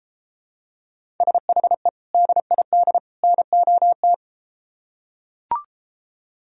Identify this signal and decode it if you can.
Morse code, heard as a keyed tone, about 32 wpm, 700 Hz. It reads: SHE DID NOT